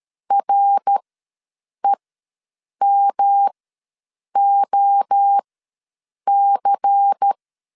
Morse code, heard as a keyed tone, about 13 wpm, spaced as for 7 wpm, 800 Hz. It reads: REMOC
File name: LEZIONI / 7 GF9Z / 7 GF ws 12wpm 3140